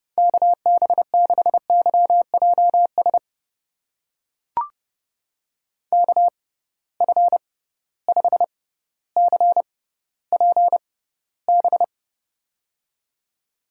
Morse code, heard as a keyed tone, about 29 wpm, spaced as for 15 wpm, 700 Hz. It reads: KB6YJS KF5CPB
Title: USA Call Signs - 6x Character Spacing - 30wpm